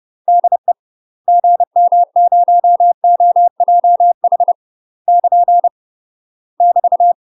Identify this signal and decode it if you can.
Morse code, heard as a keyed tone, about 30 wpm, 700 Hz. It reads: DE GM0OJH <KN> <BT>
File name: UK MASTER QSOs TRAINING/30WPM